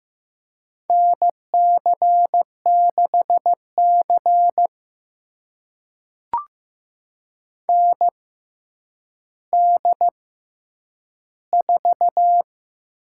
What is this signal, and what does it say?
Morse code, heard as a keyed tone, about 15 wpm, spaced as for 9 wpm, 700 Hz. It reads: NC6C ND4
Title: USA Call Signs - 6x Character Spacing - 15wpm